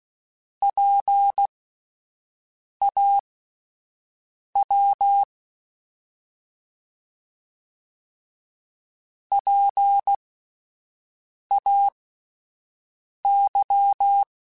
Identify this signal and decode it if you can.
Morse code, heard as a keyed tone, about 16 wpm, spaced as for 5 wpm, 800 Hz. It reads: PAW PAY